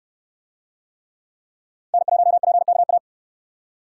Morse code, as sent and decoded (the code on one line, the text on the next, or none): -. .---- .--. -.- .-
N1PKA